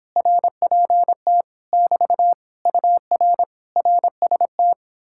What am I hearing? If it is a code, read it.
Morse code, heard as a keyed tone, about 31 wpm, 700 Hz. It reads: RPT <BT> UR RST